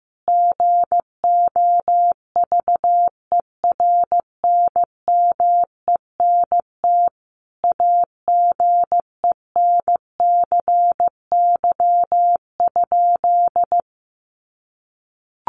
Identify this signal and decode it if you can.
Morse code, heard as a keyed tone, about 15 wpm, 700 Hz. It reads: GOVERNMENT AGENCY?
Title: Sentences from Top 1300 Words 15wpm